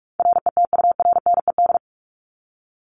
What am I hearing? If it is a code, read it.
Morse code, heard as a keyed tone, about 44 wpm, 700 Hz. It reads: RETURNED
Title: ED Suffix - Top 100 Words with Base Words 45wpm